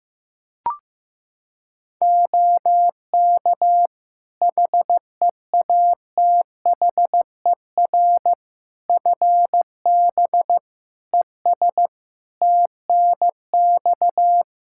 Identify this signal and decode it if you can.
Morse code, heard as a keyed tone, about 16 wpm, 700 Hz. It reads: OK HEATHER FB ES TNX
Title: K1USN SST QSO Practice 15wpm